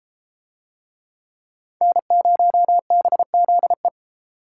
Morse code, heard as a keyed tone, about 29 wpm, 700 Hz. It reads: N0BZE